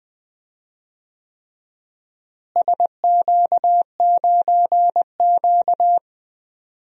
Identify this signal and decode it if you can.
Morse code, heard as a keyed tone, about 20 wpm, 700 Hz. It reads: SQ9Q
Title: DXCC-100 Call Signs - 4 Characters - Repeated 3x - 20wpm